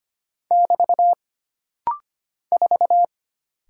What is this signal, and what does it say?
Morse code, heard as a keyed tone, about 25 wpm, 700 Hz. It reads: <BT> 4
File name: Single Character (Letter:Number:Punctuation:Prosign) - Rapid-Fire - 25wpm